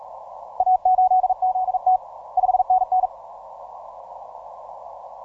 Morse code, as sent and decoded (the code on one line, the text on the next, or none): .- ---.. --.. - / ..... -. -.
A8ZT 5NN